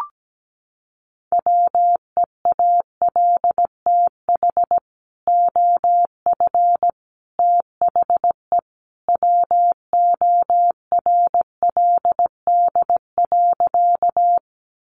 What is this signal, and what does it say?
Morse code, heard as a keyed tone, about 17 wpm, 700 Hz. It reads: WEALTH OF THE WORLD.
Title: Sentences from Top 1200 Words 17wpm